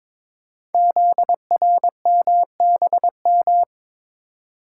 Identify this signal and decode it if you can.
Morse code, heard as a keyed tone, about 22 wpm, 700 Hz. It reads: ZRMBM
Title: Code Groups - 5 Letters - 22wpm